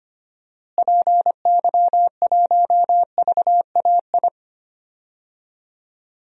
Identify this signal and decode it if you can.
Morse code, heard as a keyed tone, about 25 wpm, 700 Hz. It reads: PY1VAI
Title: DXCC-100 Call Signs - 6 Characters - Repeated 2x - 25wpm